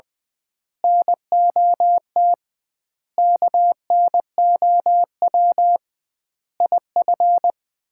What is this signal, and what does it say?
Morse code, heard as a keyed tone, about 21 wpm, 700 Hz. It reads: NOT KNOW IF
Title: 2x Word-Spacing - Sets of 5 Words - 20wpm